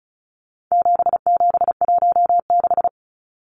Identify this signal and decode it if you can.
Morse code, heard as a keyed tone, about 34 wpm, 700 Hz. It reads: 7716
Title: Numbers - 4-Digit 35wpm